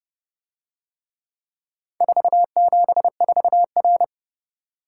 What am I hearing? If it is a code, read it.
Morse code, heard as a keyed tone, about 30 wpm, 700 Hz. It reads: 474R